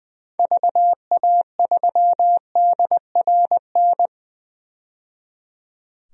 Morse code, heard as a keyed tone, about 20 wpm, 700 Hz. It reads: VA3DRN
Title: DXCC-100 Call Signs - 6 Characters - 20wpm